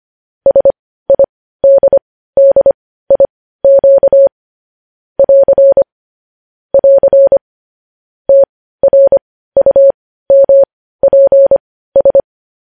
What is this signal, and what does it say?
Morse code, heard as a keyed tone, about 25 wpm, spaced as for 15 wpm, 550 Hz. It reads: SIDDIQ <AR> <AR> TRUMPS
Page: news headlines in morse code at 15 wpm